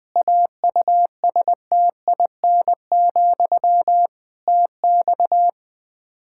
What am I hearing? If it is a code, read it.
Morse code, heard as a keyed tone, about 20 wpm, 700 Hz. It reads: AUSTIN, TX
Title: Practice QSO - Exchange 1 - Beginning - 20wpm